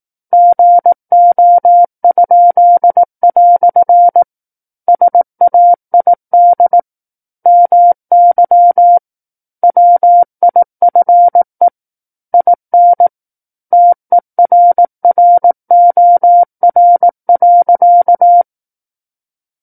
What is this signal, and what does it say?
Morse code, heard as a keyed tone, about 18 wpm, 700 Hz. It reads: GO?" SAID MY WIFE IN TERROR.
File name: 18 WPM